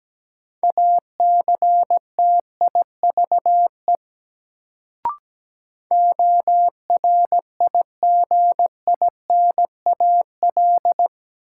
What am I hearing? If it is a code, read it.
Morse code, heard as a keyed tone, about 17 wpm, 700 Hz. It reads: ACTIVE ORIGINAL
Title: Sentences from Top 900 Words - New Word Drill - 17wpm